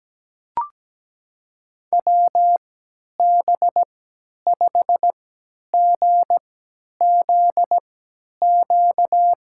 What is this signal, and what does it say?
Morse code, heard as a keyed tone, about 17 wpm, spaced as for 10 wpm, 700 Hz. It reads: WB5GZQ
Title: USA Call Signs - 6 Characters - 3x Character Spacing - 17wpm